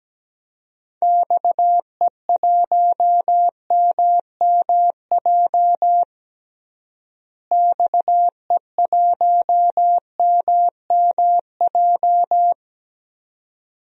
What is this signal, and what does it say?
Morse code, heard as a keyed tone, about 17 wpm, 700 Hz. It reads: XE1MMJ XE1MMJ
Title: DXCC-100 Call Signs - 6 Characters - Repeated 2x - 17wpm